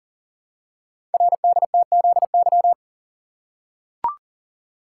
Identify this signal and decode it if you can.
Morse code, heard as a keyed tone, about 42 wpm, 700 Hz. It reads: RDTZY
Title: Code Groups - 5 Letters - 40wpm